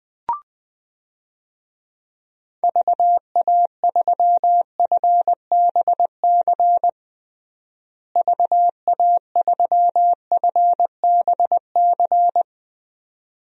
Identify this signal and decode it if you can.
Morse code, heard as a keyed tone, about 21 wpm, 700 Hz. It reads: VA3FBC VA3FBC
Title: DXCC-100 Call Signs - 6 Characters - Repeated 2x - 20wpm